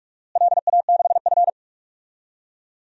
Morse code, heard as a keyed tone, about 46 wpm, 700 Hz. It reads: LA6F
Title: DXCC-100 Call Signs - 4 Characters - Repeated 3x - 45wpm